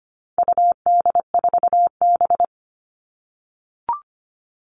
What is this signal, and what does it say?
Morse code, heard as a keyed tone, about 25 wpm, 700 Hz. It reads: UD4B
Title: All Cut Numbers - 4-Digit 25wpm